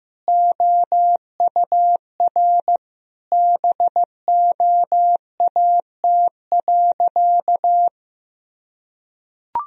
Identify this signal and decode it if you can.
Morse code, heard as a keyed tone, about 15 wpm, 700 Hz. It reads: OUR BOAT.
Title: Sentences from Top 1000 Words - Review Part 4 - 15wpm